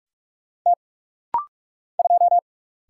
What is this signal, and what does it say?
Morse code, heard as a keyed tone, about 44 wpm, 700 Hz. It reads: T 2
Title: Single Letter-Number - Rapid-Fire 45wpm